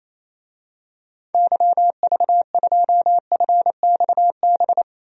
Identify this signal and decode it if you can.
Morse code, heard as a keyed tone, about 28 wpm, 700 Hz. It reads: YV2FXB
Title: DXCC-100 Call Signs - 6 Characters - Repeated 3x - 28wpm